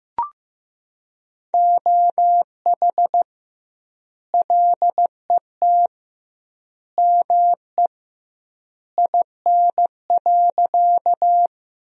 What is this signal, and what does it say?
Morse code, heard as a keyed tone, about 15 wpm, 700 Hz. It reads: OH LET ME IN.
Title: 2x Word-Spacing - Sentences from Top 500 Words - 15wpm